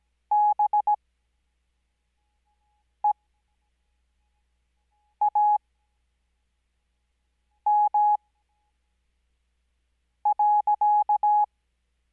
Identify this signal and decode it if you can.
Morse code, heard as a keyed tone, about 17 wpm, spaced as for 4 wpm, 840 Hz. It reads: BEAM.
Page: Farnsworth Spacing: 15WPM Character speed Spaced at 5WPM.